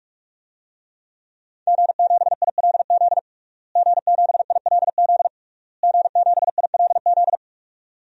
Morse code, heard as a keyed tone, about 46 wpm, 700 Hz. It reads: G7ILZ G7ILZ G7ILZ
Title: DXCC-100 Call Signs - 5 Characters - Repeated 3x - 45wpm